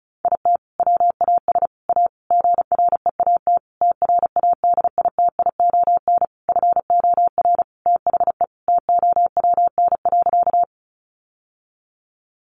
Morse code, heard as a keyed tone, about 34 wpm, 700 Hz. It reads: IT WAS A GREAT TRADITION FOR THE TOWN.
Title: Sentences from Top 1500 Words 35wpm